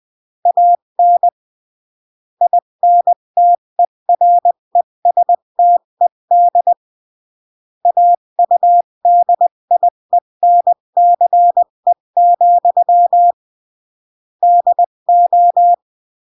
Morse code, heard as a keyed tone, about 20 wpm, 700 Hz. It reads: AN INTERESTED AUDIENCE, DO
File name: CW / jodorowsky / 82 commandments 20-15